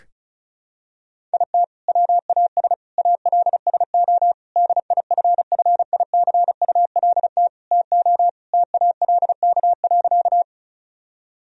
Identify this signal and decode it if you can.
Morse code, heard as a keyed tone, about 36 wpm, 700 Hz. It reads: IT WAS ALSO DIFFICULT TO TALK.
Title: Sentences from Top 1000 Words - Review Part 1 - 35wpm